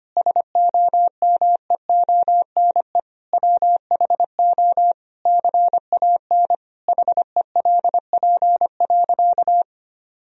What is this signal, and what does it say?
Morse code, heard as a keyed tone, about 25 wpm, 700 Hz. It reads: SOMEONE WHO CAN HELP.
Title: Sentences from Top 1000 Words - Review Part 4 - 25wpm